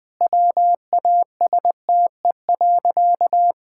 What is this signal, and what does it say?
Morse code, heard as a keyed tone, about 20 wpm, 700 Hz. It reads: WASTE.
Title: Sentences with 3-Word Phrases - Spoken Beginning - 20wpm